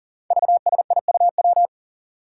Morse code, heard as a keyed tone, about 40 wpm, 700 Hz. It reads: VSIUW
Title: Five Challenging Letter-Number Combinations 40wpm